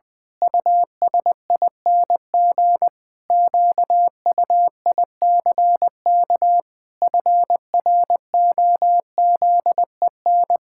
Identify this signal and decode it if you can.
Morse code, heard as a keyed tone, about 20 wpm, 700 Hz. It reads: USING QUICK FROZEN
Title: Pangram Challenge v2 - 20wpm